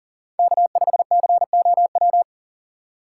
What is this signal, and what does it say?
Morse code, heard as a keyed tone, about 38 wpm, 700 Hz. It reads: K5COW
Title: USA Call Signs - 5 Characters - 3x Character Spacing - 40wpm